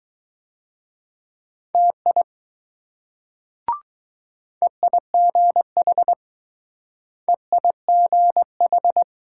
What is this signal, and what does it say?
Morse code, heard as a keyed tone, about 20 wpm, 700 Hz. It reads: TI EIGH EIGH